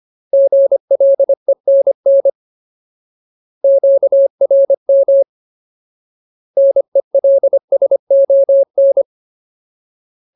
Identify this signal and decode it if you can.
Morse code, heard as a keyed tone, about 25 wpm, 550 Hz. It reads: GLENN QRM NELSON
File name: Ultimate Word List / 25WPM Ultimate